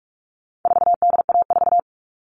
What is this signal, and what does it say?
Morse code, heard as a keyed tone, about 44 wpm, 700 Hz. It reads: 4DA4
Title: All Cut Numbers - 4-Digit 45wpm